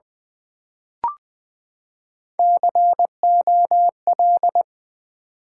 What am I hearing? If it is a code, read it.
Morse code, heard as a keyed tone, about 20 wpm, 700 Hz. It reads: COL
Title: Build A Word - 3 Letters At A Time - 9-Letter Words - 20wpm